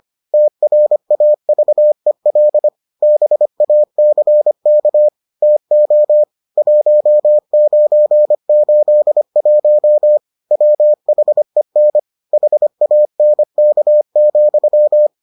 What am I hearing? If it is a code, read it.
Morse code, heard as a keyed tone, about 25 wpm, 600 Hz. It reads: TRAVEL BACK TO 1981 WHEN HANK,